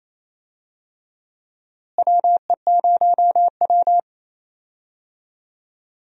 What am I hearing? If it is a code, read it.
Morse code, heard as a keyed tone, about 28 wpm, 700 Hz. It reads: WE0W
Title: USA Call Signs - 4 Characters - Repeated 3x - 28wpm